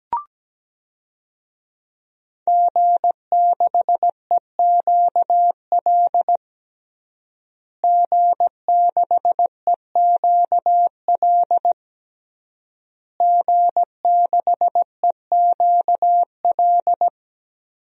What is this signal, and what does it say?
Morse code, heard as a keyed tone, about 17 wpm, 700 Hz. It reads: G6EQL G6EQL G6EQL
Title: DXCC-100 Call Signs - 5 Characters - Repeated 3x - 17wpm